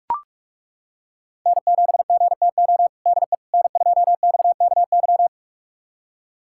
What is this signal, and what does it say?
Morse code, heard as a keyed tone, about 41 wpm, 700 Hz. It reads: N7GTO DE N2XKY
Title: USA Callsign DE Callsign - 5 Characters - 45wpm